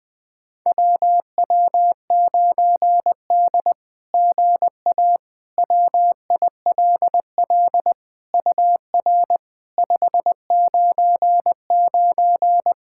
Morse code, heard as a keyed tone, about 20 wpm, 700 Hz. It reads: WW9D GA WILL UR 599
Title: SOTA QSO Practice 20wpm